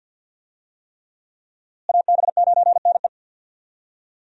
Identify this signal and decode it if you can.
Morse code, heard as a keyed tone, about 47 wpm, 700 Hz. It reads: AB9NE
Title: USA Call Signs - 5 Characters - Repeated 3x - 50wpm